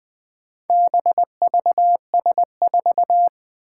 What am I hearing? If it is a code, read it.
Morse code, heard as a keyed tone, about 20 wpm, 700 Hz. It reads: BVS4
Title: Four Challenging Letter-Number Combinations 20wpm